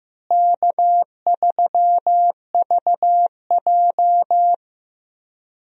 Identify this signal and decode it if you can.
Morse code, heard as a keyed tone, about 15 wpm, 700 Hz. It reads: K3VJ
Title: USA Call Signs - 4 Characters - Repeated 3x - 15wpm